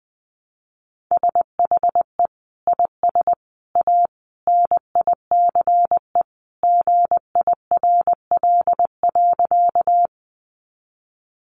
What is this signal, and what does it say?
Morse code, heard as a keyed tone, about 20 wpm, 700 Hz. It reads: SHE IS A NICE GIRL.
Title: Sentences from Top 1100 Words 20wpm